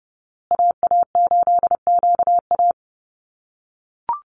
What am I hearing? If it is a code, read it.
Morse code, heard as a keyed tone, about 30 wpm, 700 Hz. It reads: AA8QA
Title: USA Call Signs Repeated 3x 30wpm